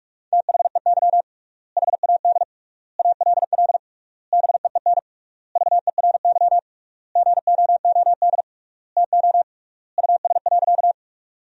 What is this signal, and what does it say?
Morse code, heard as a keyed tone, about 47 wpm, 700 Hz. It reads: THEY HAD ALL BEEN VERY GOOD TO US.
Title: 3x Word-Spacing - Sentences from Top 200 Words - 45wpm